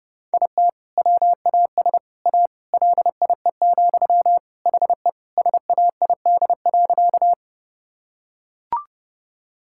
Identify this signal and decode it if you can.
Morse code, heard as a keyed tone, about 30 wpm, 700 Hz. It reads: IT WAS A LIE, HE SAID.
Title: Sentences from Top 1600 Words 30wpm